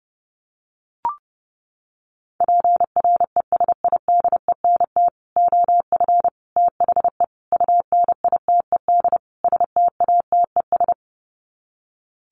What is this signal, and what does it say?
Morse code, heard as a keyed tone, about 30 wpm, 700 Hz. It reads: PRESIDENT OF THE UNITED STATES
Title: Sets of 5 Words 30wpm